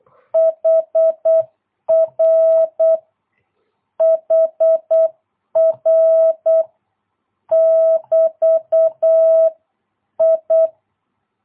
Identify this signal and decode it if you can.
Morse code, heard as a keyed tone, about 8 wpm, 660 Hz. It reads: HR HR <BT> I
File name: jh8cbh / 2022cw